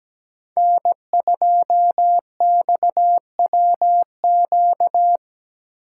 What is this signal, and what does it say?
Morse code, heard as a keyed tone, about 20 wpm, 700 Hz. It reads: N2XWQ
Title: Speed Racing - USA 1x3 Call Signs 17wpm